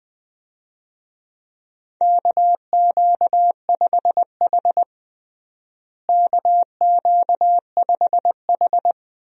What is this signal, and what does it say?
Morse code, heard as a keyed tone, about 21 wpm, 700 Hz. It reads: KQ5H KQ5H
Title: USA Call Signs - 4 Characters - Repeated 3x - 20wpm